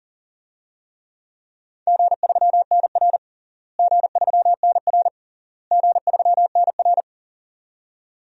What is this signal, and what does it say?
Morse code, heard as a keyed tone, about 40 wpm, 700 Hz. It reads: G3NR G3NR G3NR
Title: DXCC-100 Call Signs - 4 Characters - Repeated 3x - 40wpm